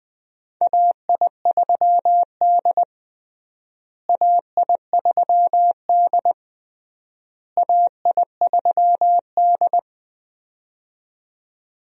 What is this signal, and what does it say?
Morse code, heard as a keyed tone, about 20 wpm, 700 Hz. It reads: AI3D AI3D AI3D
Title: USA Call Signs - 4 Characters - Repeated 3x - 20wpm